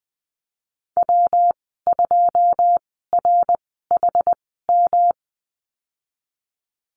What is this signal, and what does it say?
Morse code, heard as a keyed tone, about 20 wpm, spaced as for 15 wpm, 700 Hz. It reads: W2RHM
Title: 193 Lesson 31g - Letter v - Call Signs 20wpm